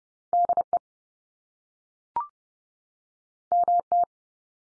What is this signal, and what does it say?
Morse code, heard as a keyed tone, about 30 wpm, 700 Hz. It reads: DE MT
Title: US States 30WPM